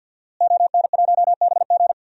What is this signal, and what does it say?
Morse code, heard as a keyed tone, about 47 wpm, 700 Hz. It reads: ON1BG
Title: DXCC-100 Call Signs - 5 Characters - Repeated 3x - 50wpm